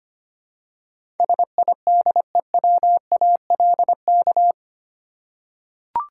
Morse code, heard as a keyed tone, about 25 wpm, 700 Hz. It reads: SIDEWALK
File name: Eight Letter Words - 25wpm